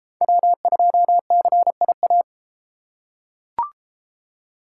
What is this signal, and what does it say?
Morse code, heard as a keyed tone, about 31 wpm, 700 Hz. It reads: W2CIA